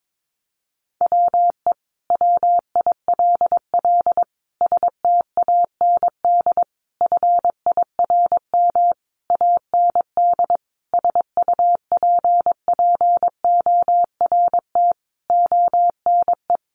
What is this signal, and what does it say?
Morse code, heard as a keyed tone, about 22 wpm, 700 Hz. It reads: WE WILL STAND FIRM AND SUPPORT ONE
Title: Sentences from Top 1400 Words 22wpm